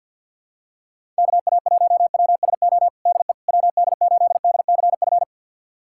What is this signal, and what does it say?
Morse code, heard as a keyed tone, about 51 wpm, 700 Hz. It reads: KA1WSO DE WD8DCF
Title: USA Callsign DE Callsign - 6 Characters - 50wpm